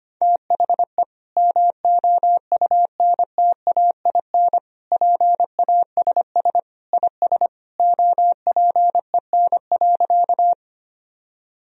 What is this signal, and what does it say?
Morse code, heard as a keyed tone, about 25 wpm, 700 Hz. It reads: THE MOUNTAIN PASS IS OPEN.
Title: Sentences from Top 1000 Words - Review Part 1 - 25wpm